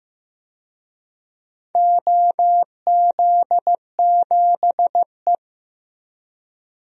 Morse code, heard as a keyed tone, about 15 wpm, 700 Hz. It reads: OZ7E
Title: DXCC-100 Call Signs - 4 Characters - Repeated 3x - 15wpm